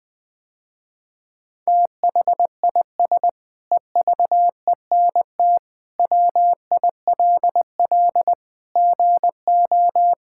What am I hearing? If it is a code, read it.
Morse code, heard as a keyed tone, about 21 wpm, 700 Hz. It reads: THIS EVENT WILL GO
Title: Sentences with 7-Word Phrases - Spoken Beginning - 20wpm